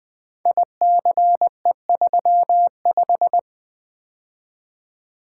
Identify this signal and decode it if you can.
Morse code, heard as a keyed tone, about 20 wpm, 700 Hz. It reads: ICE35
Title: Code Groups - 5 Letter:Numbers - 20wpm